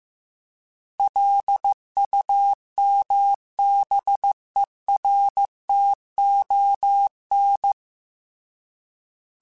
Morse code, heard as a keyed tone, about 15 wpm, 800 Hz. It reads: LUMBERTON